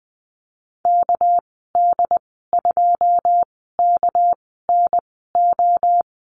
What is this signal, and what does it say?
Morse code, heard as a keyed tone, about 20 wpm, spaced as for 15 wpm, 700 Hz. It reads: KD2KNO
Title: 213 Lesson 34g - Number 8 - Call Signs 20wpm